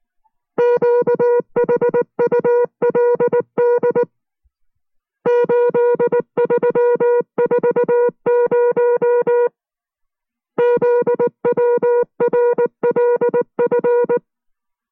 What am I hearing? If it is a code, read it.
Morse code, heard as a keyed tone, about 19 wpm, 480 Hz. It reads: QHULD 8340 ZWRLF